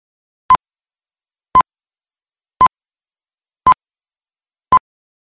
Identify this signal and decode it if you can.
There is no code.